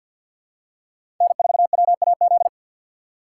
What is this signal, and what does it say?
Morse code, heard as a keyed tone, about 52 wpm, 700 Hz. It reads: N4WAZ
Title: USA Call Signs - 6x Character Spacing - 50wpm